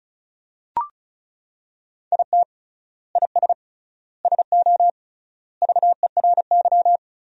Morse code, heard as a keyed tone, about 36 wpm, 700 Hz. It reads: IT IS SO VERY
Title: 3x Word-Spacing - Sentences from Top 300 Words - 35wpm